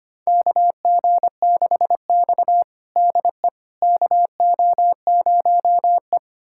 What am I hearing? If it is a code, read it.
Morse code, heard as a keyed tone, about 25 wpm, 700 Hz. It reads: KG6X DE KO0E
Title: USA Callsign DE Callsign - 4 Characters - 25wpm